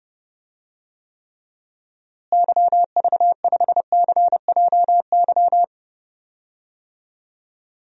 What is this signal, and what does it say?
Morse code, heard as a keyed tone, about 30 wpm, 700 Hz. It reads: YV5CJY
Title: DXCC-100 Call Signs - 6 Characters - Repeated 2x - 30wpm